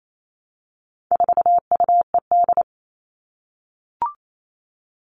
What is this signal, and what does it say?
Morse code, heard as a keyed tone, about 28 wpm, 700 Hz. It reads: 4UED
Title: All Cut Numbers - 4-Digit 28wpm